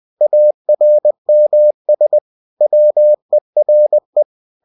Morse code, heard as a keyed tone, about 20 wpm, 600 Hz. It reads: ARMS WERE